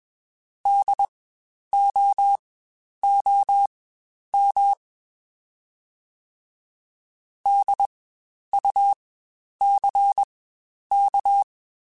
Morse code, heard as a keyed tone, about 21 wpm, spaced as for 10 wpm, 800 Hz. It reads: DOOM DUCK
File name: Spaced Alphas / Spaced Alphas mp3s / Ds sp @20